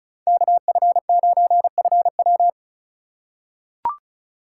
Morse code, heard as a keyed tone, about 36 wpm, 700 Hz. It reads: KF9FW
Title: USA Call Signs - 5 Characters - 3x Character Spacing - 35wpm